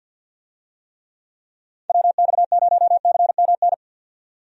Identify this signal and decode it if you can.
Morse code, heard as a keyed tone, about 53 wpm, 700 Hz. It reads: WX0CMN
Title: USA Call Signs - 6 Characters - 50wpm